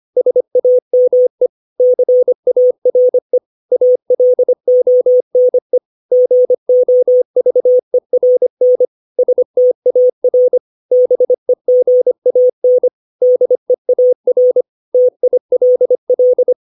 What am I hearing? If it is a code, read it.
Morse code, heard as a keyed tone, about 25 wpm, 500 Hz. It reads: SAME CARE ALONE GOVERN STAR BEGAN DEAR TILL
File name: 632 words / 632 Words 25WPM